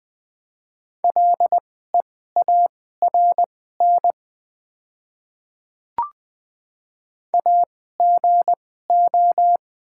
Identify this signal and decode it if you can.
Morse code, heard as a keyed tone, about 20 wpm, spaced as for 14 wpm, 700 Hz. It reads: LEARN AGO
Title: 2x Character Spacing - Top 500 Words - 20wpm